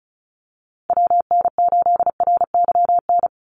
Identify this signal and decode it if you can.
Morse code, heard as a keyed tone, about 36 wpm, 700 Hz. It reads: WN8RYN
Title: USA Call Signs 35wpm